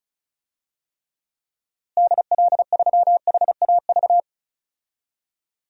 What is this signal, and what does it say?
Morse code, heard as a keyed tone, about 34 wpm, 700 Hz. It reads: DL3HAV
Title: DXCC-100 Call Signs - 6 Characters - Repeated 2x - 35wpm